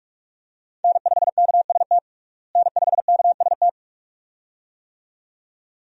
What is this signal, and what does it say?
Morse code, heard as a keyed tone, about 42 wpm, 700 Hz. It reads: N5KST N5KST
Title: USA Call Signs - 5 Characters - Repeated 2x - 45wpm